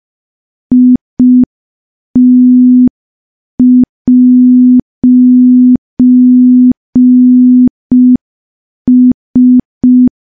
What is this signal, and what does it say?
Morse code, heard as a keyed tone, about 5 wpm, 250 Hz. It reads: IT'S
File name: morse / morse-training-05WPM